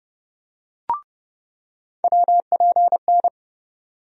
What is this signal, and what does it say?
Morse code, heard as a keyed tone, about 30 wpm, 700 Hz. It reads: WPN
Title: USA Call Sign Postfixes - 3 Letters 30wpm